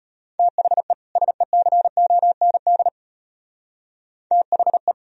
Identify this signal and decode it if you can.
Morse code, heard as a keyed tone, about 31 wpm, 700 Hz. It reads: THE SECOND THE
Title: Speed-Racing - Sets of 2 Words 25wpm